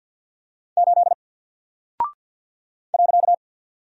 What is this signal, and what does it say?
Morse code, heard as a keyed tone, about 48 wpm, 700 Hz. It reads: 8 .
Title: Single Character (Letter:Number:Punctuation:Prosign) Alt 50wpm